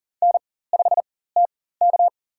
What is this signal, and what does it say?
Morse code, heard as a keyed tone, about 40 wpm, spaced as for 19 wpm, 700 Hz. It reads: N5TK
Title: USA Call Signs - 4 Characters - 4x Character Spacing - 40wpm